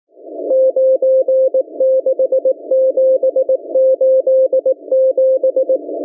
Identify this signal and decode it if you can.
Morse code, heard as a keyed tone, about 18 wpm, 520 Hz. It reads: 96787